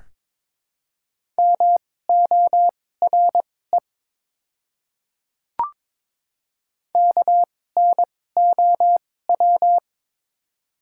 Morse code, heard as a keyed tone, about 22 wpm, spaced as for 16 wpm, 700 Hz. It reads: MORE KNOW
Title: 2x Character Spacing - Top 200 Words - 22wpm